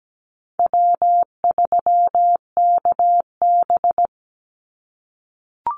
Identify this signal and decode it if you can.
Morse code, heard as a keyed tone, about 17 wpm, 700 Hz. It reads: W3KB
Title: Easy Call Signs - 17wpm